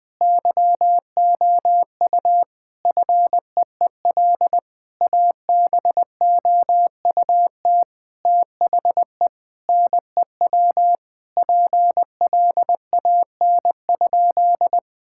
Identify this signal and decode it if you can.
Morse code, heard as a keyed tone, about 21 wpm, 700 Hz. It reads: YOU FEEL ABOUT THE NEW PLAN?
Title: Sentences with 5-Word Phrases - 20wpm